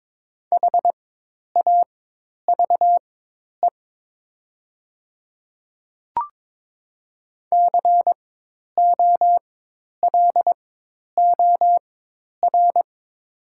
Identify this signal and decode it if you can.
Morse code, heard as a keyed tone, about 22 wpm, spaced as for 10 wpm, 700 Hz. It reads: HAVE COLOR
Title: 4x Character Spacing - Top 300 Words - 22wpm